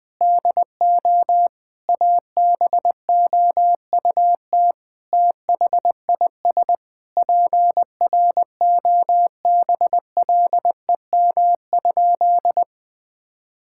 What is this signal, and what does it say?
Morse code, heard as a keyed tone, about 20 wpm, 700 Hz. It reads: DO ABOUT THIS PROBLEM?
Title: Sentences with 8-Word Phrases - 20wpm